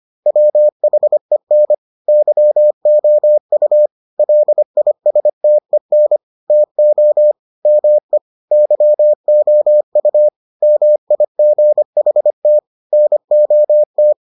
Morse code, heard as a keyed tone, about 25 wpm, 600 Hz. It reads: WHEN YOU LISTEN TO ME YOU MIGHT NOT